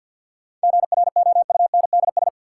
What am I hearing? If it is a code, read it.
Morse code, heard as a keyed tone, about 52 wpm, 700 Hz. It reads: GROUNDS
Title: Sentences from Top 1300 Words - New Word Drill Alt - 50wpm